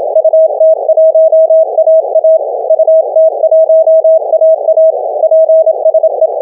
Morse code, heard as a keyed tone, about 27 wpm, 640 Hz. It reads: UT1AA UT1AA PSE